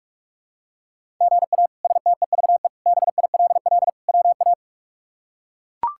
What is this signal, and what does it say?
Morse code, heard as a keyed tone, about 41 wpm, 700 Hz. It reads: GA STEVE BILL WA
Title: K1USN SST QSO Practice - 45wpm